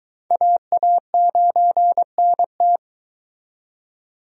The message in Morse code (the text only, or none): AA9NT